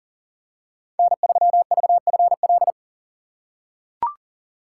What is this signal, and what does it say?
Morse code, heard as a keyed tone, about 40 wpm, 700 Hz. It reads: N3VFL
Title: USA Call Signs - 5 Characters - 6x Character Spacing - 40wpm